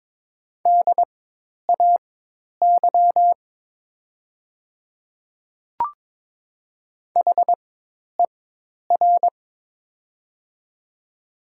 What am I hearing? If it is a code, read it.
Morse code, heard as a keyed tone, about 22 wpm, spaced as for 10 wpm, 700 Hz. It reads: DAY HER
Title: 4x Character Spacing - Top 100 Words - 22wpm